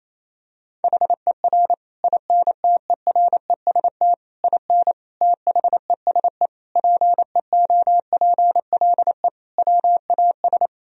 Morse code, heard as a keyed tone, about 28 wpm, 700 Hz. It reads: HER INTEREST IN THESE PEOPLE WAS
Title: Sentences from Top 1000 Words - Review Part 2 - 28wpm